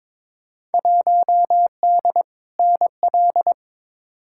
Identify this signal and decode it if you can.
Morse code, heard as a keyed tone, about 22 wpm, 700 Hz. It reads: 1D NL
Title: Field Day Exchanges Encore - 22wpm